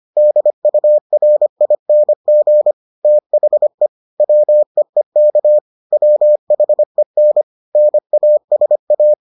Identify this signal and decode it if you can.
Morse code, heard as a keyed tone, about 25 wpm, 600 Hz. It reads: DURING THE WEEK WHEN NASA